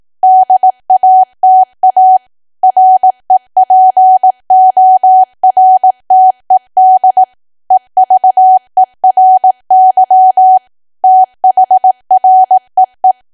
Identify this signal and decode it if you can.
Morse code, heard as a keyed tone, about 18 wpm, 750 Hz. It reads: DATA REPORTED EVERY THREE
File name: cwa / W1AW-Level2 / 18WPM